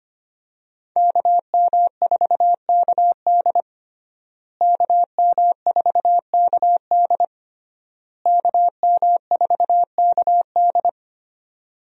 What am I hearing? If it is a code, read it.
Morse code, heard as a keyed tone, about 25 wpm, 700 Hz. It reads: KM4KD KM4KD KM4KD
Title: USA Call Signs - 5 Characters - Repeated 3x - 25wpm